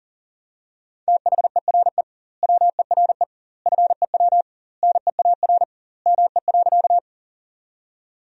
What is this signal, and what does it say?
Morse code, heard as a keyed tone, about 40 wpm, 700 Hz. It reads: THERE WERE FEW NEAR ME.
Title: 2x Word-Spacing - Sentences from Top 300 Words - 40wpm